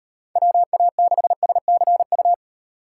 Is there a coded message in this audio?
Yes: Morse code from 0.4 to 2.3 s, heard as a keyed tone, about 37 wpm, 700 Hz. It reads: WA6SCU